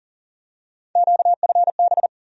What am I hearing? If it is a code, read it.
Morse code, heard as a keyed tone, about 40 wpm, 700 Hz. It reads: QFB
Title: USA Call Sign Postfixes - 3 Letters 40wpm